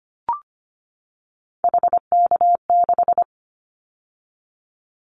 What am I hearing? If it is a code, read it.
Morse code, heard as a keyed tone, about 25 wpm, 700 Hz. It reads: HK6
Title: Callsign Prefixes for Top 100 DXCC - 25wpm